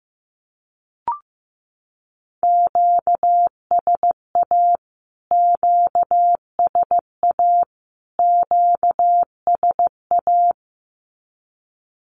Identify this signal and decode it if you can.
Morse code, heard as a keyed tone, about 15 wpm, 700 Hz. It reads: QSA QSA QSA
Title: Q-Codes Repeated 3x 15wpm